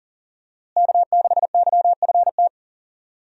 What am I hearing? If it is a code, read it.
Morse code, heard as a keyed tone, about 40 wpm, 700 Hz. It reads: K6YFT
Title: USA Call Signs - 5 Characters - Repeated 3x - 40wpm